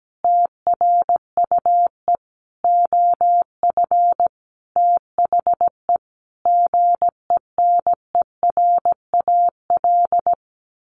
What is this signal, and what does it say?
Morse code, heard as a keyed tone, about 17 wpm, 700 Hz. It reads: TRUE OF THE GENERAL
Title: Sentences from Top 1400 Words 17wpm